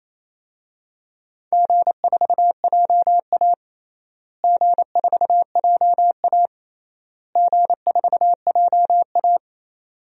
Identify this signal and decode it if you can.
Morse code, heard as a keyed tone, about 28 wpm, 700 Hz. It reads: G4JA G4JA G4JA
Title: DXCC-100 Call Signs - 4 Characters - Repeated 3x - 28wpm